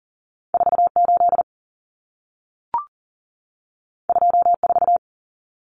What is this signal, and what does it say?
Morse code, heard as a keyed tone, about 40 wpm, 700 Hz. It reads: 48 24
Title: Numbers - 2-Digit 40wpm